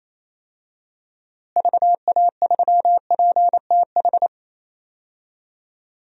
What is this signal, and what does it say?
Morse code, heard as a keyed tone, about 28 wpm, 700 Hz. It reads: VA3PTH
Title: DXCC-100 Call Signs - 6 Characters - Repeated 2x - 28wpm